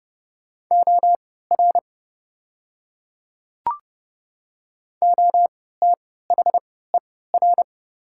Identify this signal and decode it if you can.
Morse code, heard as a keyed tone, about 30 wpm, spaced as for 17 wpm, 700 Hz. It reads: OR OTHER
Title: 3x Character Spacing - Top 100 Words - 30wpm